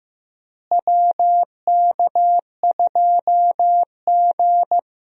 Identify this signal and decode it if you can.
Morse code, heard as a keyed tone, about 15 wpm, 700 Hz. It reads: WK2G
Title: USA Call Signs - 4 Characters - Repeated 3x - 15wpm